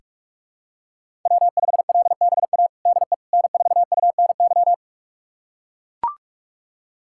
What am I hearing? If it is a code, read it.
Morse code, heard as a keyed tone, about 41 wpm, 700 Hz. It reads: W5LBA DE N4UNY
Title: USA Callsign DE Callsign - 5 Characters - 45wpm